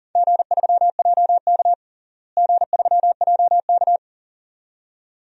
Morse code, heard as a keyed tone, about 39 wpm, 700 Hz. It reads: G3JK G3JK
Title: DXCC-100 Call Signs - 4 Characters - Repeated 2x - 40wpm